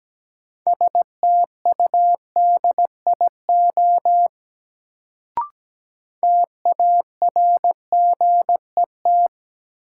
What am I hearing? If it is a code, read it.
Morse code, heard as a keyed tone, about 17 wpm, 700 Hz. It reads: STUDIO TARGET
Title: Sentences from Top 1200 Words - New Word Drill - 17wpm